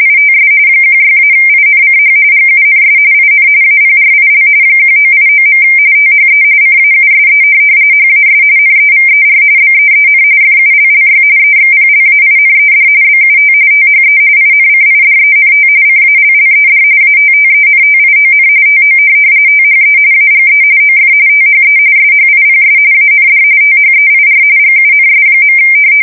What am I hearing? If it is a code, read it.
RTTY, about 45.45 baud. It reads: ITS 16-INCH GUNS IN APRIL 1989. THE CAUSE HAS NEVER BEEN FULLY EXPLAINED BUT THE INCIDENT CALLED ATTENTION TO THE AGE OF THE SHIPS AND THEIR GUNS.
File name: fsk170 / USS-New-Jersey